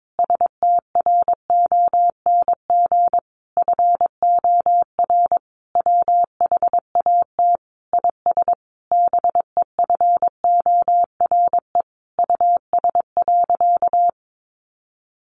Morse code, heard as a keyed tone, about 22 wpm, 700 Hz. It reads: STRONG FOR WHAT IS BEFORE US.
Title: Sentences from Top 600 Words 22wpm